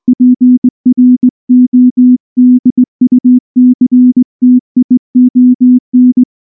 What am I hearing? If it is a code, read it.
Morse code, heard as a keyed tone, about 18 wpm, 250 Hz. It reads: PRODUCTION
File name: morse-training-20WPM